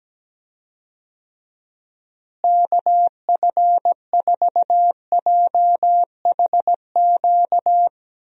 Morse code, heard as a keyed tone, about 17 wpm, 700 Hz. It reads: KF4JHQ
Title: USA Call Signs - 6 Characters - Repeated 3x - 17wpm